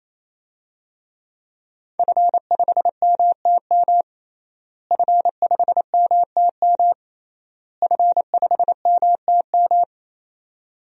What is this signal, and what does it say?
Morse code, heard as a keyed tone, about 27 wpm, 700 Hz. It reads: F5MTM F5MTM F5MTM
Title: DXCC-100 Call Signs - 5 Characters - Repeated 3x - 28wpm